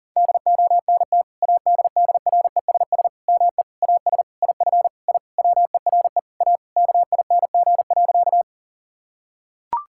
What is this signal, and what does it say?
Morse code, heard as a keyed tone, about 40 wpm, 700 Hz. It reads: DONT ADDRESS ME AS IF I WERE A KING.
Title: Sentences from Top 1000 Words - Review Part 6 - 40wpm